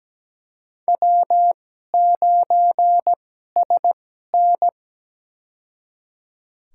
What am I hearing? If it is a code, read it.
Morse code, heard as a keyed tone, about 17 wpm, spaced as for 12 wpm, 700 Hz. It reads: W9SN
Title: USA Call Signs - 4 Characters - 2x Character Spacing - 17wpm